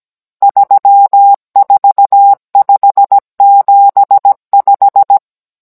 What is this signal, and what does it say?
Morse code, heard as a keyed tone, about 17 wpm, 800 Hz. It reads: 34575
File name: Cours-NR18-16mm